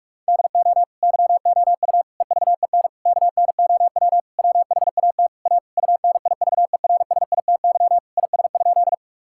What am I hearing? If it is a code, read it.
Morse code, heard as a keyed tone, about 46 wpm, 700 Hz. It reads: DO YOU EVEN KNOW WHAT A UNIVERSITY IS?
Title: Sentences from Top 1000 Words - Review Part 5 - 45wpm